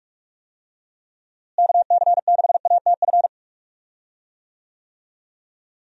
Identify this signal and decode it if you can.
Morse code, heard as a keyed tone, about 43 wpm, 700 Hz. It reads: KC6ATF
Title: USA Call Signs - 6 Characters - Repeated 3x - 45wpm